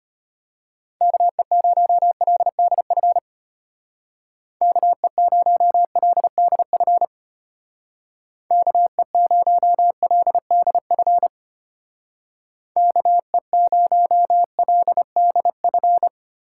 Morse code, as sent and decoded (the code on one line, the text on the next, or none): -.- . ----- .-.. -.. ..-. / -.- . ----- .-.. -.. ..-. / -.- . ----- .-.. -.. ..-. / -.- . ----- .-.. -.. ..-.
KE0LDF KE0LDF KE0LDF KE0LDF